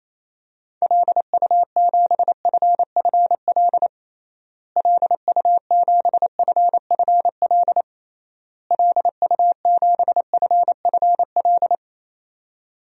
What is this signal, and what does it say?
Morse code, heard as a keyed tone, about 27 wpm, 700 Hz. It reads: LU7FFL LU7FFL LU7FFL
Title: DXCC-100 Call Signs - 6 Characters - Repeated 3x - 28wpm